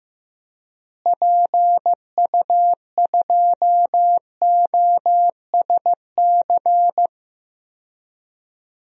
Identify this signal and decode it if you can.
Morse code, heard as a keyed tone, about 15 wpm, 700 Hz. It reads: PU2OSC
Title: DXCC-100 Call Signs - 6 Characters - Repeated 2x - 15wpm